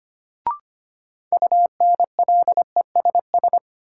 Morse code, heard as a keyed tone, about 25 wpm, 700 Hz. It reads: UNLESS
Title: Sentences from Top 1000 Words - Review Part 5 - New Word Drill - 25wpm